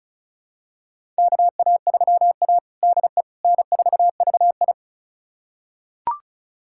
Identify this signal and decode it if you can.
Morse code, heard as a keyed tone, about 36 wpm, 700 Hz. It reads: KA3A DE N4VI
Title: USA Callsign DE Callsign - 4 Characters - 35wpm